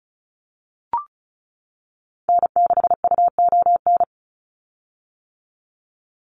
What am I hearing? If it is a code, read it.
Morse code, heard as a keyed tone, about 34 wpm, 700 Hz. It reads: N6UON
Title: USA Call Signs 35wpm